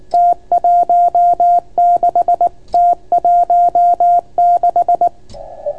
Morse code, heard as a keyed tone, about 24 wpm, 690 Hz. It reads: T16 T16